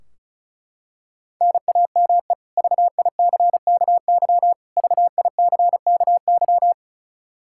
Morse code, heard as a keyed tone, about 34 wpm, 700 Hz. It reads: NAME VICKY VICKY
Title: K1USN SST QSO Practice 35wpm